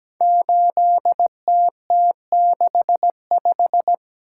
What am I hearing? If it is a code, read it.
Morse code, heard as a keyed tone, about 17 wpm, 700 Hz. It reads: 8TT65
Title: Code Groups - 5 Letter:Numbers - 17wpm